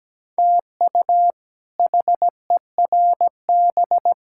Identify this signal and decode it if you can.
Morse code, heard as a keyed tone, about 17 wpm, 700 Hz. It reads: TU HERB
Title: K1USN SST QSO Practice - 17wpm